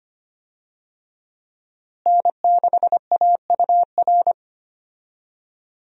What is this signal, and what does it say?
Morse code, heard as a keyed tone, about 25 wpm, 700 Hz. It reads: N6AUR
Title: USA Call Signs - 5 Characters - Repeated 3x - 25wpm